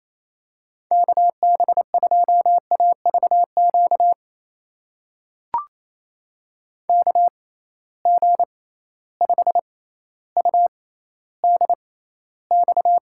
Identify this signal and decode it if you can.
Morse code, heard as a keyed tone, about 28 wpm, spaced as for 14 wpm, 700 Hz. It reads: KB2AVQ KG5UDX
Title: USA Call Signs - 6 Characters - 6x Character Spacing - 28wpm